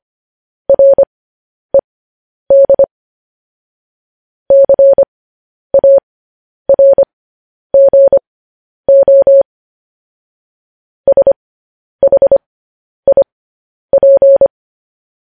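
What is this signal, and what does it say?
Morse code, heard as a keyed tone, about 25 wpm, spaced as for 10 wpm, 550 Hz. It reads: RED CARGO SHIP
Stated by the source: Morse code sending of news headlines, speed 10.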